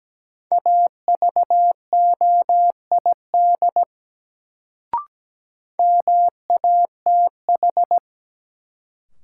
Morse code, heard as a keyed tone, about 17 wpm, 700 Hz. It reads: AVOID MATH
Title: Sentences from Top 1000 Words - Review Part 6 - New Word Drill - 17wpm